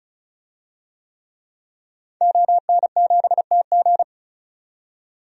Morse code, heard as a keyed tone, about 36 wpm, 700 Hz. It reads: ON7TG
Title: DXCC-100 Call Signs - 5 Characters - 35wpm